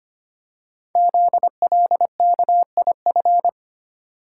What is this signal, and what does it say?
Morse code, heard as a keyed tone, about 25 wpm, 700 Hz. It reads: ZLKIF
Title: Code Groups - 5 Letters - 25wpm